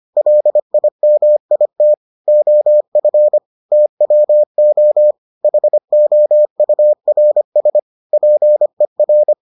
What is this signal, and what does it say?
Morse code, heard as a keyed tone, about 25 wpm, 600 Hz. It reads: LIMIT OF TWO HOURS PER